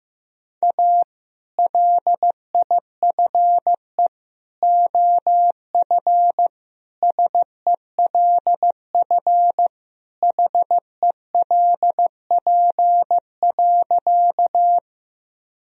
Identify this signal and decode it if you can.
Morse code, heard as a keyed tone, about 15 wpm, 700 Hz. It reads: A LIFE OF SELF HELP.
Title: Sentences from Top 1000 Words - Review Part 1 - 15wpm